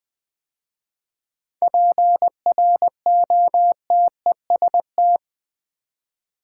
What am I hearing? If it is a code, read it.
Morse code, heard as a keyed tone, about 20 wpm, 700 Hz. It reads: PROTEST
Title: 7 Letter Words - First 3 Letters Given - 20wpm